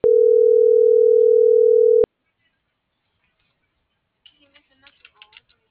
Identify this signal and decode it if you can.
Ringback tone (North America), 440 + 480 Hz.